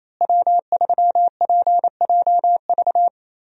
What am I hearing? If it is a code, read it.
Morse code, heard as a keyed tone, about 28 wpm, 700 Hz. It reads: W3PJV
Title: USA Call Signs - 5 Characters - Repeated 3x - 28wpm